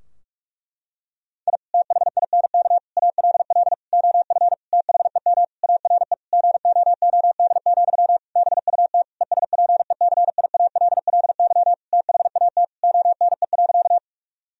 Morse code, heard as a keyed tone, about 46 wpm, 700 Hz. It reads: I THINK ALL OF THEM ARE GOOD, BUT ESPECIALLY THAT ONE.
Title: Sentences from Top 1000 Words - Review Part 4 - 45wpm